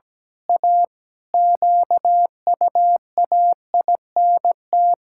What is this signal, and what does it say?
Morse code, heard as a keyed tone, about 17 wpm, 700 Hz. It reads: A QUAINT
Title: Pangram Challenge v2 - 17wpm